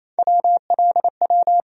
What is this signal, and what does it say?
Morse code, heard as a keyed tone, about 28 wpm, 700 Hz. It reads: WLW
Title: USA Call Sign Postfixes - 3 Letters 28wpm